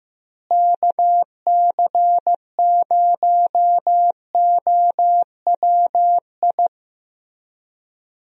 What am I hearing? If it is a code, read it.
Morse code, heard as a keyed tone, about 15 wpm, 700 Hz. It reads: KC0OWI
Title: USA Call Signs - 6 Characters - Repeated 3x - 15wpm